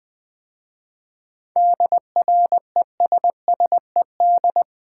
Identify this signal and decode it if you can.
Morse code, heard as a keyed tone, about 20 wpm, 700 Hz. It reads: DRESSED
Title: 7 Letter Words - First 3 Letters Given - 20wpm